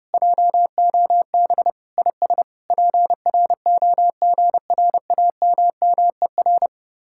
Morse code, heard as a keyed tone, about 32 wpm, 700 Hz. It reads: JOB IS PROGRAMMER